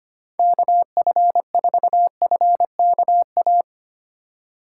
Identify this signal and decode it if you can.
Morse code, heard as a keyed tone, about 25 wpm, 700 Hz. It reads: KF4FKA
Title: USA Call Signs - 6 Characters - 4x Character Spacing - 25wpm